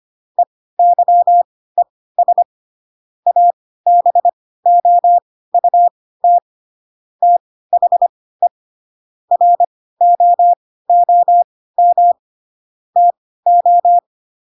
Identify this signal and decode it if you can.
Morse code, heard as a keyed tone, about 25 wpm, spaced as for 16 wpm, 700 Hz. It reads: EYES ABOUT THE ROOM TO
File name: Pinocchio / Pinocchio ch1 10 25 ik1hge